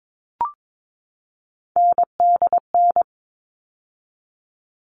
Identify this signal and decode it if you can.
Morse code, heard as a keyed tone, about 22 wpm, 700 Hz. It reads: NDN
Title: All Cut Numbers - 3-Digit 22wpm